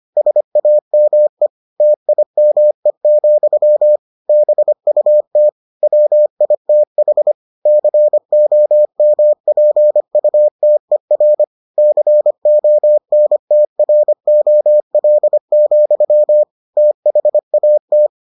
Morse code, heard as a keyed tone, about 25 wpm, 600 Hz. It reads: SAME TIME, BUT WITH COMPUTER CONTROL, THAT